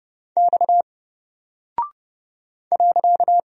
Morse code, heard as a keyed tone, about 30 wpm, 700 Hz. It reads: X .
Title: Single Character (Letter:Number:Punctuation:Prosign) Alt 30wpm